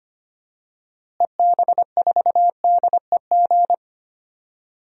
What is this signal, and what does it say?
Morse code, heard as a keyed tone, about 25 wpm, 700 Hz. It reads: EB4DEG
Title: DXCC-100 Call Signs - 6 Characters - Repeated 2x - 25wpm